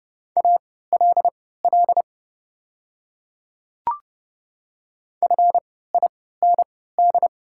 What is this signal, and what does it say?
Morse code, heard as a keyed tone, about 30 wpm, spaced as for 17 wpm, 700 Hz. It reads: ALL FIND
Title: 3x Character Spacing - Top 100 Words - 30wpm